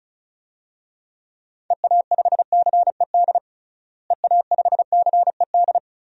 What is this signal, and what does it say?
Morse code, heard as a keyed tone, about 36 wpm, 700 Hz. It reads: EA5CED EA5CED
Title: DXCC-100 Call Signs - 6 Characters - Repeated 3x - 35wpm